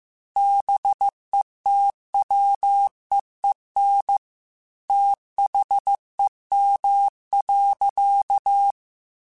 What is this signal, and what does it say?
Morse code, heard as a keyed tone, about 15 wpm, 800 Hz. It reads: BETWEEN THEM.
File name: Technical Subjects / Technical Subjects-mp3s / Sw Pwr Sup @13